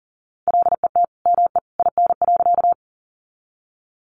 Morse code, heard as a keyed tone, about 40 wpm, 700 Hz. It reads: LET ME IN.
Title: Sentences from Top 300 Words 40wpm